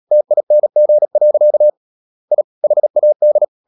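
Morse code, heard as a keyed tone, about 37 wpm, 600 Hz. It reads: TING. I HAD
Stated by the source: Morse code practice at 35–39 wpm sending English text of Ken Follett's novel Never, from CW med Gnister.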